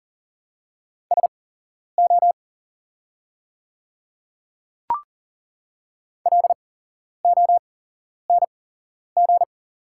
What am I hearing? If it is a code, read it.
Morse code, heard as a keyed tone, about 40 wpm, spaced as for 11 wpm, 700 Hz. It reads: SO LONG
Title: 8x Character Spacing - Top 100 Words - 40wpm